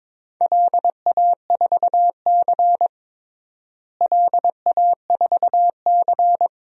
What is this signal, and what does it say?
Morse code, heard as a keyed tone, about 22 wpm, 700 Hz. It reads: LA4C LA4C
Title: DXCC-100 Call Signs - 4 Characters - Repeated 2x - 22wpm